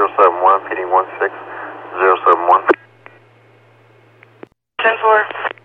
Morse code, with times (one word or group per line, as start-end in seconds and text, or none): none